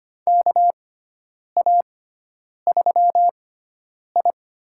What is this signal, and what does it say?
Morse code, heard as a keyed tone, about 25 wpm, spaced as for 9 wpm, 700 Hz. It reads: KA3I
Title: USA Call Signs - 6x Character Spacing - 25wpm